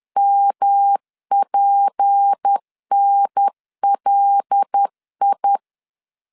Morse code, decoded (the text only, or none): MPNLI